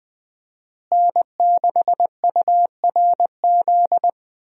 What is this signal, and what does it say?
Morse code, heard as a keyed tone, about 20 wpm, 700 Hz. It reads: N6URZ
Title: USA Call Signs - 5 Characters - Repeated 2x - 20wpm